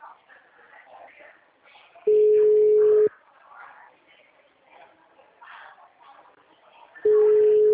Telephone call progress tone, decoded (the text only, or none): ringback tone